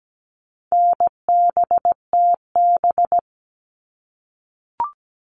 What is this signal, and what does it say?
Morse code, heard as a keyed tone, about 17 wpm, 700 Hz. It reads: NBTB
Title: All Cut Numbers - 4-Digit 17wpm